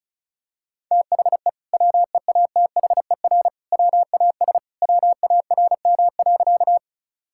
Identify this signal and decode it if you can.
Morse code, heard as a keyed tone, about 36 wpm, 700 Hz. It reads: THE WEATHER WAS WARM.
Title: Sentences from Top 1000 Words - Review Part 5 - 35wpm